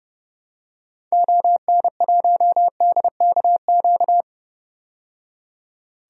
Morse code, heard as a keyed tone, about 32 wpm, 690 Hz. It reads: ON1DKQ
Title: DXCC-100 Call Signs - 6 Characters - 30wpm